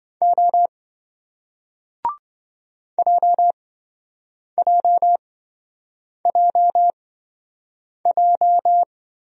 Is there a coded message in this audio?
Yes: Morse code from 0.2 to 8.8 s, heard as a keyed tone, about 27 wpm, spaced as for 7 wpm, 700 Hz. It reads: O JJJJ